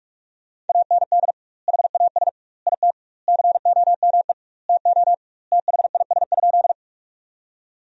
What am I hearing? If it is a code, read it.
Morse code, heard as a keyed tone, about 46 wpm, 700 Hz. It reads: AND HAS IT COME TO THIS?
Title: 2x Word-Spacing - Sentences from Top 100 Words - 45wpm